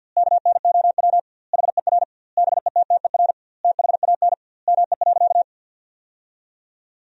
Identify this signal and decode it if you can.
Morse code, heard as a keyed tone, about 50 wpm, 700 Hz. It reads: KNOW HER BETTER THAN ME.
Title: 2x Word-Spacing - Sentences from Top 400 Words - 50wpm